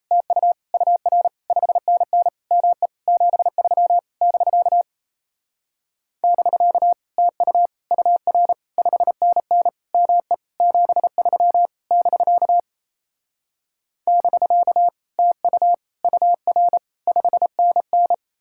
Morse code, heard as a keyed tone, about 37 wpm, 700 Hz. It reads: TU UR 5NN ME 73 <BK> <BK> TU UR 5NN ME 73 <BK> <BK> TU UR 5NN